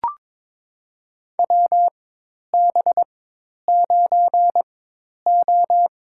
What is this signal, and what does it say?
Morse code, heard as a keyed tone, about 22 wpm, spaced as for 10 wpm, 700 Hz. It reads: WB9O
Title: USA Call Signs - 4 Characters - 4x Character Spacing - 22wpm